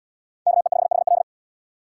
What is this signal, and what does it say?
Morse code, heard as a keyed tone, about 77 wpm, 700 Hz. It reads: KE5HL